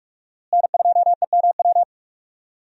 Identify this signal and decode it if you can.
Morse code, heard as a keyed tone, about 41 wpm, 700 Hz. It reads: N2EMW
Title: USA Call Signs - 5 Characters - Repeated 2x - 45wpm